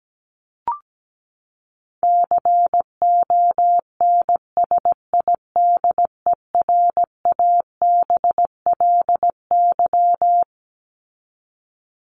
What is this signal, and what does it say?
Morse code, heard as a keyed tone, about 17 wpm, 700 Hz. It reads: CONSIDERABLY
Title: LY Suffix - Top 100 Words with Base Words 17wpm